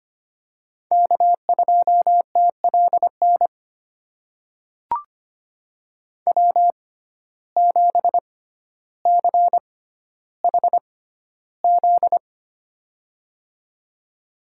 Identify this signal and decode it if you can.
Morse code, heard as a keyed tone, about 25 wpm, spaced as for 13 wpm, 700 Hz. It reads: K2TLN W7CHZ
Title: USA Call Signs - 5 Characters - 6x Character Spacing - 25wpm